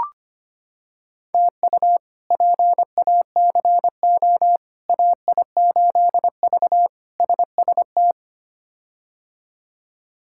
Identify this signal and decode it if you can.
Morse code, heard as a keyed tone, about 25 wpm, 700 Hz. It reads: TU PACO AI8V SST
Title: K1USN SST QSO Practice - 25wpm